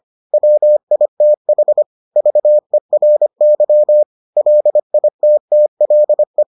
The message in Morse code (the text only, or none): WITH VERY LITTLE